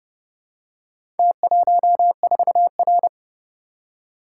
Morse code, heard as a keyed tone, about 28 wpm, 700 Hz. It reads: T14R